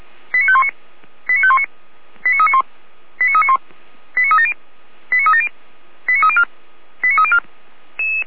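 Five-tone selective call (ZVEI), heard as quick runs of five tones, tones of about 80 ms. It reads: 89519 89519 89391 89391 89370 89370 89304 89304
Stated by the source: Five-tone ZVEI selective call 89518, received on SDR.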